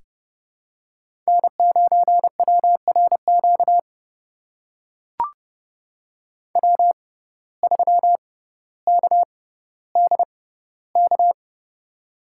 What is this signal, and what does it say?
Morse code, heard as a keyed tone, about 30 wpm, spaced as for 15 wpm, 700 Hz. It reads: N9WRQ W3KDK
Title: USA Call Signs - 5 Characters - 6x Character Spacing - 30wpm